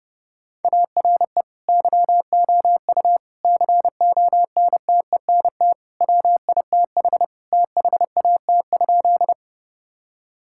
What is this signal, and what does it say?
Morse code, heard as a keyed tone, about 30 wpm, 700 Hz. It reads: ARE YOU CONTENT WITH THAT?
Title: Sentences from Top 1000 Words - Review Part 6 - 30wpm